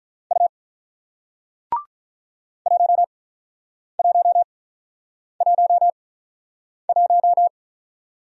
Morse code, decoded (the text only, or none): U 1111